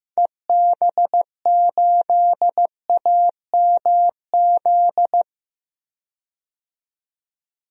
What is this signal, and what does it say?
Morse code, heard as a keyed tone, about 15 wpm, 700 Hz. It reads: EB8AMZ
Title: DXCC-100 Call Signs - 6 Characters - 15wpm